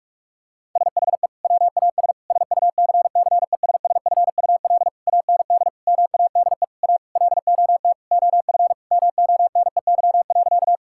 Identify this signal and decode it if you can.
Morse code, heard as a keyed tone, about 46 wpm, 700 Hz. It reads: SHE WAS SUCCESSFUL AND MADE A LOT OF MONEY.
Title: Sentences from Top 1000 Words - Review Part 6 - 45wpm